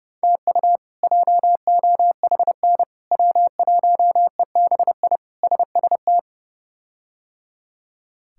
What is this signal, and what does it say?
Morse code, heard as a keyed tone, about 30 wpm, 700 Hz. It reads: TU JOHN W1EBI SST
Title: K1USN SST QSO Practice - 30wpm